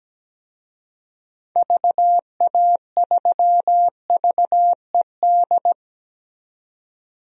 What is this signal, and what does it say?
Morse code, heard as a keyed tone, about 17 wpm, 700 Hz. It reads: VA3VED
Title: DXCC-100 Call Signs - 6 Characters - 17wpm